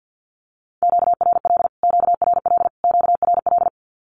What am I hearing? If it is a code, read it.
Morse code, heard as a keyed tone, about 48 wpm, 700 Hz. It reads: QRL QRL QRL
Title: Q-Codes Repeated 3x 50wpm